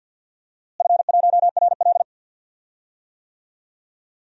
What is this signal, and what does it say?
Morse code, heard as a keyed tone, about 47 wpm, 700 Hz. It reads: F1RL
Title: DXCC-100 Call Signs - 4 Characters - Repeated 2x - 50wpm